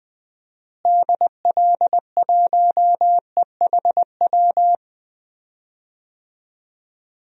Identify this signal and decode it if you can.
Morse code, heard as a keyed tone, about 20 wpm, 690 Hz. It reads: DL1EHW
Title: DXCC-100 Call Signs - 6 Characters - Repeated 2x - 20wpm